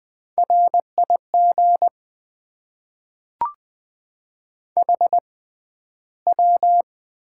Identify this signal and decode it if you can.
Morse code, heard as a keyed tone, about 20 wpm, spaced as for 12 wpm, 700 Hz. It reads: RIG HW
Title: Top 20 QSO Elements - 6x Character Spacing - 20wpm